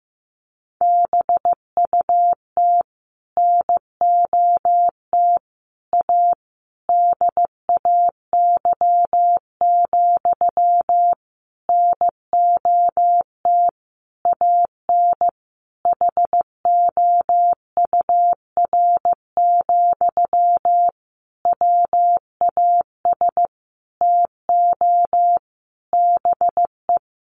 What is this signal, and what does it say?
Morse code, heard as a keyed tone, about 15 wpm, 700 Hz. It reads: BUT NOT A DAY, NOT AN HOUR, WAS TO BE
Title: Sentences from Top 1400 Words 15wpm